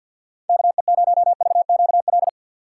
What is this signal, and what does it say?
Morse code, heard as a keyed tone, about 47 wpm, 700 Hz. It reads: KE0VQL
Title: USA Call Signs - 3x Character Spacing - 50wpm